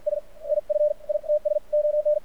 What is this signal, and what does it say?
Morse code, heard as a keyed tone, about 51 wpm, 600 Hz. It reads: S IF ITS ON
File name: archived / CRSnet